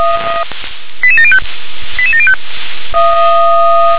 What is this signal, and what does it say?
Five-tone selective call (ZVEI), heard as quick runs of five tones, tones of about 70 ms. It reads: 88705 88705
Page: SDR/ZVEI